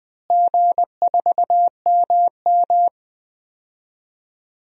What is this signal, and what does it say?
Morse code, heard as a keyed tone, about 21 wpm, 700 Hz. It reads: G4MM
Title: DXCC-100 Call Signs - 4 Characters - Repeated 2x - 20wpm